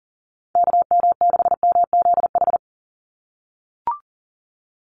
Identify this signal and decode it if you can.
Morse code, heard as a keyed tone, about 40 wpm, 700 Hz. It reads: KM6MZH
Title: USA Call Signs 40wpm